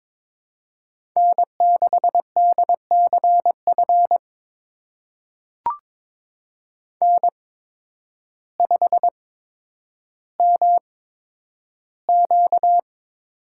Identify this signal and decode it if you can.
Morse code, heard as a keyed tone, about 22 wpm, spaced as for 10 wpm, 700 Hz. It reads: N6DCF N5MQ
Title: USA Call Signs - 5 Characters - 8x Character Spacing - 22wpm